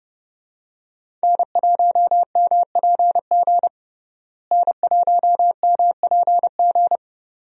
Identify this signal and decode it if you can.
Morse code, heard as a keyed tone, about 31 wpm, 700 Hz. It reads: N1MPG N1MPG
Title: USA Call Signs - 5 Characters - Repeated 2x - 30wpm